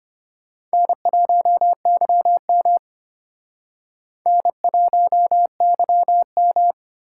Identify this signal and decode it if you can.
Morse code, heard as a keyed tone, about 32 wpm, 700 Hz. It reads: N1YM N1YM